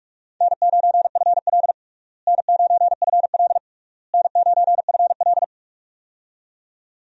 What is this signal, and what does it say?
Morse code, heard as a keyed tone, about 42 wpm, 700 Hz. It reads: N9FL N9FL N9FL
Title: USA Call Signs - 4 Characters - Repeated 3x - 45wpm